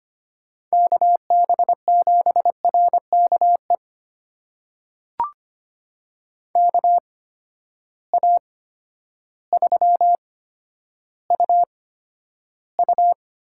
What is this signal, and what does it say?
Morse code, heard as a keyed tone, about 25 wpm, spaced as for 11 wpm, 700 Hz. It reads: KB7RKE KA3UU
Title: USA Call Signs - 6 Characters - 8x Character Spacing - 25wpm